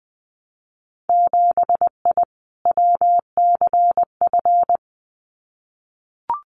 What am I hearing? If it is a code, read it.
Morse code, heard as a keyed tone, about 20 wpm, 700 Hz. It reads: 7I WCF
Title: Winter Field Day Exchanges Encore 20wpm